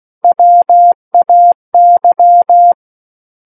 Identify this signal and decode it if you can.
Morse code, heard as a keyed tone, about 16 wpm, 700 Hz. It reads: WAY
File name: Book One / 07-Chapter 7 - 16 WPM